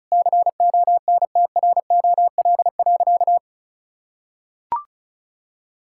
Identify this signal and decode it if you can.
Morse code, heard as a keyed tone, about 36 wpm, 690 Hz. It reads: CONTROL.
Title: Sentences from Top 1000 Words - Review Part 5 - 35wpm